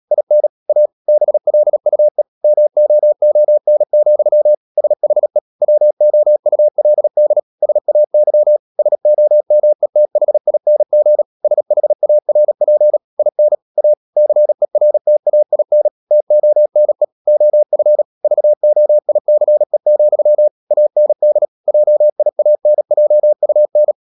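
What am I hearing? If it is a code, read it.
Morse code, heard as a keyed tone, about 37 wpm, 600 Hz. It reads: IN A BLUE MOON, SHE WOULD SAY SOMETHING SHARP IN A CERTAIN TONE OF VOICE, AND JIANJUN
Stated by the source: Morse code practice at 35–39 wpm sending English text of Ken Follett's novel Never, from CW med Gnister.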